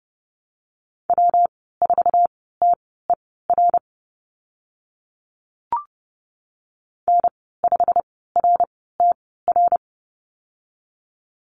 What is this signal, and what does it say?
Morse code, heard as a keyed tone, about 30 wpm, spaced as for 17 wpm, 700 Hz. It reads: W4TER N5RTR
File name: Lesson 19g - Letter w - Call Signs 30wpm